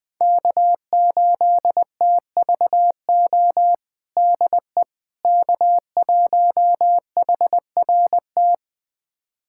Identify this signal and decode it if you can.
Morse code, heard as a keyed tone, about 21 wpm, 700 Hz. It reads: K8TVO DE K1HRT
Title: USA Callsign DE Callsign - 5 Characters - 20wpm